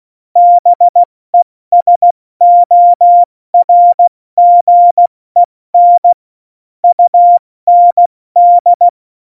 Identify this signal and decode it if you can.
Morse code, heard as a keyed tone, about 16 wpm, 700 Hz. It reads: BESORGEN UND
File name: Morsekurs / Witze / Witz 20. Oktober 2020, 70 F80